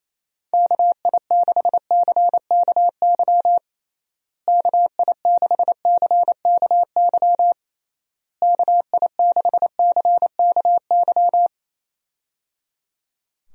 Morse code, heard as a keyed tone, about 28 wpm, 700 Hz. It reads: KI6CKY KI6CKY KI6CKY
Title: USA Call Signs - 6 Characters - Repeated 3x - 28wpm